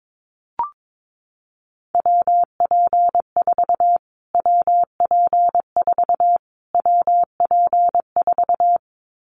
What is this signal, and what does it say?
Morse code, heard as a keyed tone, about 22 wpm, 700 Hz. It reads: WP4 WP4 WP4
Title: Callsign Prefixes for Top 25 DXCC - Repeated 3x - 22wpm